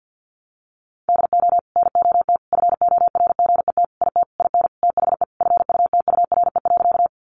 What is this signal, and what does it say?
Morse code, heard as a keyed tone, about 50 wpm, 700 Hz. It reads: DO NOT FORGET IT IN THE FUTURE.
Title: Sentences from Top 1200 Words 50wpm